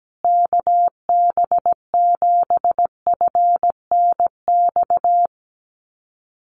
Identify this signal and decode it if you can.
Morse code, heard as a keyed tone, about 17 wpm, 700 Hz. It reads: KB7FNX
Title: USA Call Signs Repeated 2x 17wpm